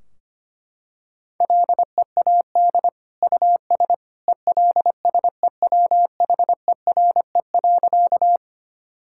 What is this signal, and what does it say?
Morse code, heard as a keyed tone, about 25 wpm, 700 Hz. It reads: LEAD US ELSEWHERE.
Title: Sentences from Top 1600 Words 25wpm